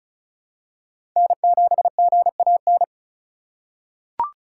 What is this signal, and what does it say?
Morse code, heard as a keyed tone, about 36 wpm, 700 Hz. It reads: N7GAN
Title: USA Call Signs - 5 Characters - 6x Character Spacing - 35wpm